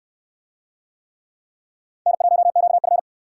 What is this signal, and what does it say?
Morse code, heard as a keyed tone, about 67 wpm, 700 Hz. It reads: N1QF